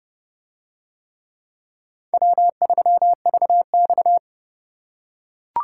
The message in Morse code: W3VX

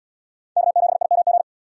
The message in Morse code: K7IMG

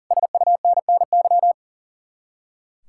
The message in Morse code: SUNNY

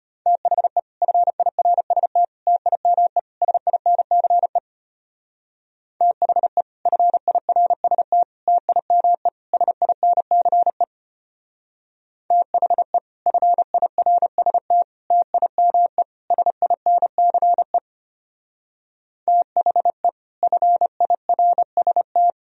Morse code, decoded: THE FIRST TIME SINCE THE FIRST TIME SINCE THE FIRST TIME SINCE THE FIRST

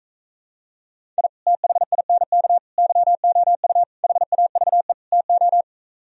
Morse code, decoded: I THINK YOU HAVE TO

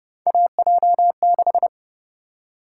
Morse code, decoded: AJ6

